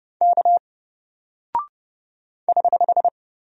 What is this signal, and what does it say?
Morse code, heard as a keyed tone, about 30 wpm, 700 Hz. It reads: K <HH>
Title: Single Character (Letter:Number:Punctuation:Prosign) Alt 30wpm